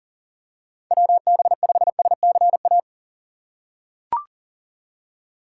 Morse code, heard as a keyed tone, about 40 wpm, 700 Hz. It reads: WB5SCA
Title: USA Call Signs - 3x Character Spacing - 40wpm